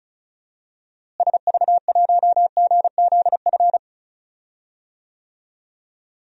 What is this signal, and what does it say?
Morse code, heard as a keyed tone, about 34 wpm, 700 Hz. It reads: SV1GZF
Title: DXCC-100 Call Signs - 6 Characters - Repeated 3x - 35wpm